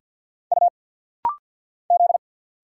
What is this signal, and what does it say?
Morse code, heard as a keyed tone, about 50 wpm, 700 Hz. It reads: U Z
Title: Single Letters - Rapid-Fire 50wpm